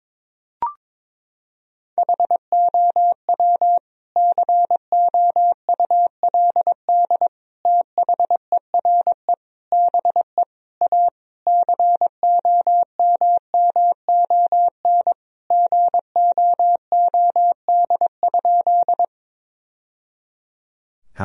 Morse code, heard as a keyed tone, about 21 wpm, 700 Hz. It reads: HOW COULD THERE BE A COMMON GOOD?
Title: Sentences from Top 1000 Words - Review Part 3 - 22wpm